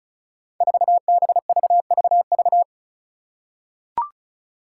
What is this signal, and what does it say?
Morse code, heard as a keyed tone, about 34 wpm, 700 Hz. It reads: 4BVVV
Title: Five Challenging Letter-Number Combinations 35wpm